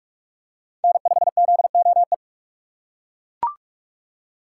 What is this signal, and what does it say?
Morse code, heard as a keyed tone, about 42 wpm, 700 Hz. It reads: N5ZOE